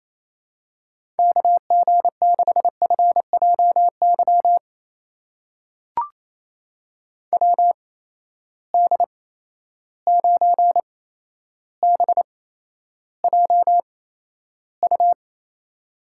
Morse code, heard as a keyed tone, about 28 wpm, spaced as for 12 wpm, 700 Hz. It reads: KG6FJY WD9BJU
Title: USA Call Signs - 6 Characters - 8x Character Spacing - 28wpm